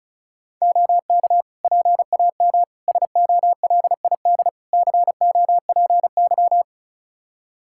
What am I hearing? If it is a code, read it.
Morse code, heard as a keyed tone, about 34 wpm, 700 Hz. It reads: OK PAM SOLID COPY
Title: K1USN SST QSO Practice 35wpm